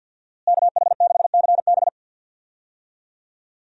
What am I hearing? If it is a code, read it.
Morse code, heard as a keyed tone, about 52 wpm, 700 Hz. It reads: KH6CB
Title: USA Call Signs - 5 Characters - Repeated 2x - 50wpm